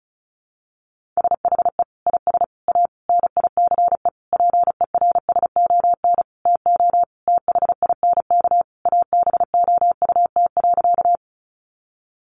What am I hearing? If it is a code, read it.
Morse code, heard as a keyed tone, about 34 wpm, 700 Hz. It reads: SHE IS A NICE PERSON TO THINK ABOUT.
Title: Sentences from Top 1100 Words 35wpm